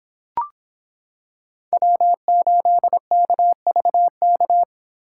Morse code, heard as a keyed tone, about 26 wpm, 700 Hz. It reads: W8KVK